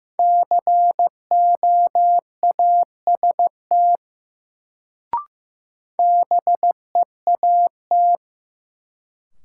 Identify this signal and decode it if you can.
Morse code, heard as a keyed tone, about 16 wpm, 700 Hz. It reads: COAST BEAT
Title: Sentences from Top 1300 Words - New Word Drill Alt - 15wpm